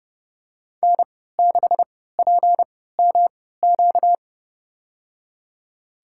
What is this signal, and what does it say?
Morse code, heard as a keyed tone, about 31 wpm, spaced as for 17 wpm, 700 Hz. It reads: N6PMQ
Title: USA Call Signs - 5 Characters - 3x Character Spacing - 30wpm